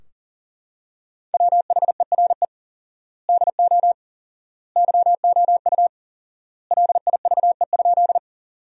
Morse code, heard as a keyed tone, about 40 wpm, 700 Hz. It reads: WHERE DO YOU LIVE?
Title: 4x Word-Spacing - Sentences from Top 200 Words - 40wpm